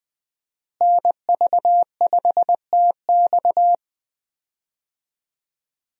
Morse code, heard as a keyed tone, about 20 wpm, 700 Hz. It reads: NV5TX
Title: USA Call Signs - 5 Characters - Repeated 2x - 20wpm